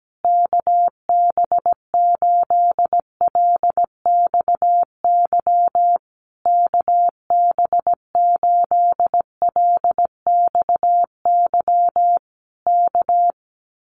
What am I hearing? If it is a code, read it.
Morse code, heard as a keyed tone, about 17 wpm, 700 Hz. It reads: KB8LXY KB8LXY K
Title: Common QSO Phrases Encore 17wpm